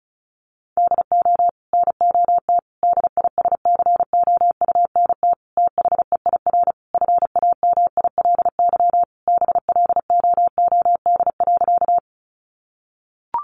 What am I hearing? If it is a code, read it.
Morse code, heard as a keyed tone, about 34 wpm, 700 Hz. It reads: DO NOT DISCOUNT THEIR FAMILY BLOOD.
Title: Sentences from Top 1200 Words 35wpm